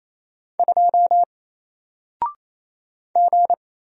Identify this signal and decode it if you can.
Morse code, heard as a keyed tone, about 27 wpm, 700 Hz. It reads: 2 G
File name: Single-Letter-Number-28wpm